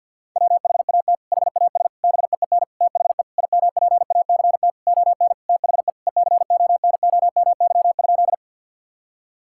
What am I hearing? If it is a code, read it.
Morse code, heard as a keyed tone, about 50 wpm, 700 Hz. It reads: WHAT HAS BEEN THE IMPACT ON THE ECONOMY?
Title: Sentences from Top 1000 Words - Review Part 6 - 50wpm